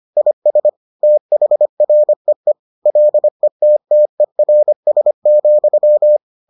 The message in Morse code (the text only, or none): IS THREE LETTERS,